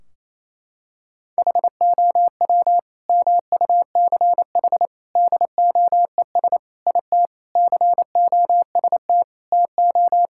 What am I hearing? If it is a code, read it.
Morse code, heard as a keyed tone, about 28 wpm, 700 Hz. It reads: HOW MUCH DOES IT COST TO
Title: Sentences from Top 1600 Words 28wpm